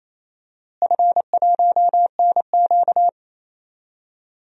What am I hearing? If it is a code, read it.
Morse code, heard as a keyed tone, about 27 wpm, 700 Hz. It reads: F1NQ
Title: DXCC-100 Call Signs - 4 Characters - Repeated 2x - 28wpm